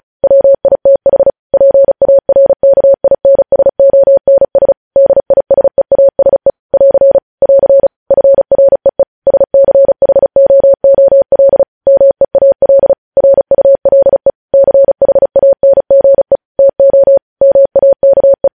Morse code, heard as a keyed tone, about 36 wpm, 550 Hz. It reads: WITH PARKINSONS DISEASE <AR> <AR> FREE SCHOOL MEAL RULE CHANGE TO MAKE
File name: podcast / News headlines-1x-35x35